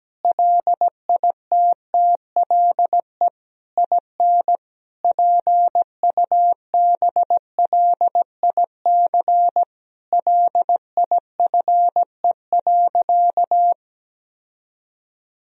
Morse code, heard as a keyed tone, about 17 wpm, 700 Hz. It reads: LITTLE IN PUBLIC LIFE.
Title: Sentences from Top 1000 Words - Review Part 3 - 17wpm